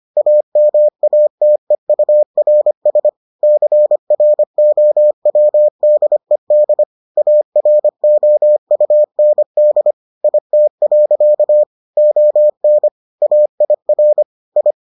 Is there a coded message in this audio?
Yes: Morse code from 0.2 to 14.7 s, heard as a keyed tone, about 25 wpm, 600 Hz. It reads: AMATEURS CROWDED AROUND IT. ON AIR I